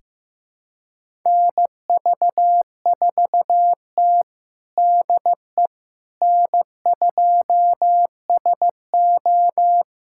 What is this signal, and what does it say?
Morse code, heard as a keyed tone, about 15 wpm, 700 Hz. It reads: NV4T DE N2SO
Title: USA Callsign DE Callsign - 4 Characters - 15wpm